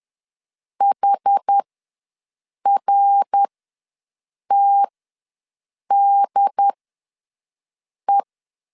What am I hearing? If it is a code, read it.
Morse code, heard as a keyed tone, about 10 wpm, spaced as for 6 wpm, 800 Hz. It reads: HRTDE
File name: LEZIONI / 5 LBVC / 5 L ws 10wpm 3636